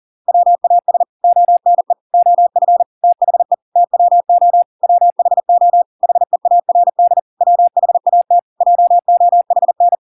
Morse code, heard as a keyed tone, about 40 wpm, 700 Hz. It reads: WAS ONE OF THE TWO WHO HEARD WHAT JOHN